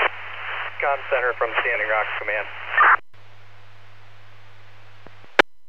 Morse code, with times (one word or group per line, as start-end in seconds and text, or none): none